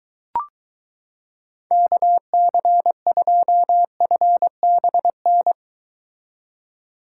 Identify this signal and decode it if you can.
Morse code, heard as a keyed tone, about 23 wpm, 700 Hz. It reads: KC2FBN